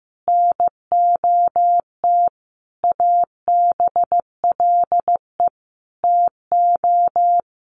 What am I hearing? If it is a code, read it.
Morse code, heard as a keyed tone, about 14 wpm, 700 Hz. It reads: NOT ABLE TO
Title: Sentences from Top 1300 Words 15wpm